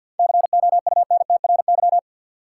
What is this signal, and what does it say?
Morse code, heard as a keyed tone, about 53 wpm, 700 Hz. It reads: COUNTRY
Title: Sentences from Top 300 Words - New Word Drill - 50wpm